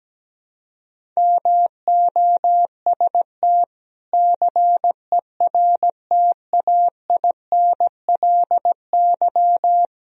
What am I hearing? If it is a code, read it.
Morse code, heard as a keyed tone, about 17 wpm, 700 Hz. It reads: MOST CERTAINLY
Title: Sentences from Top 1000 Words - Review Part 1 - 17wpm